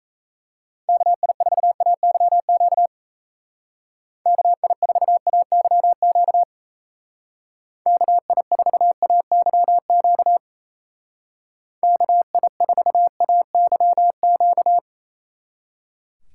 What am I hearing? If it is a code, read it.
Morse code, heard as a keyed tone, about 37 wpm, 700 Hz. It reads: KI4AYQ KI4AYQ KI4AYQ KI4AYQ